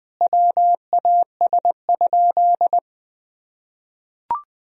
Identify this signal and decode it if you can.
Morse code, heard as a keyed tone, about 20 wpm, 700 Hz. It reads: WAS?
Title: Sentences from Top 1000 Words - Review Part 4 - 20wpm